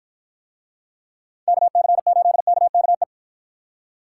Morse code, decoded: KC8KKE

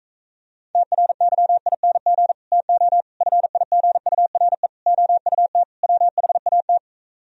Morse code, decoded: TRYING TO FIGURE OUT WHAT

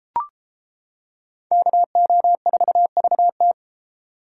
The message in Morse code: KO4VT

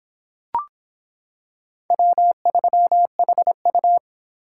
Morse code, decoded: W3HU

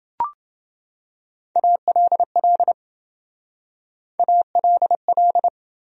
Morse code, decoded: ALL ALL

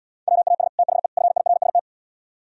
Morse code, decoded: WAS THE FIRST

